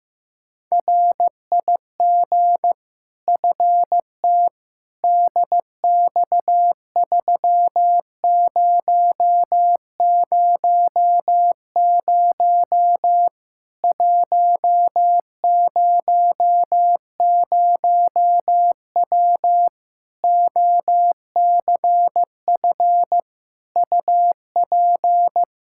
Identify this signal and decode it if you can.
Morse code, heard as a keyed tone, about 16 wpm, 700 Hz. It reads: RIG FT DX3000 100W OCF UP
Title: K1USN SST QSO Practice 15wpm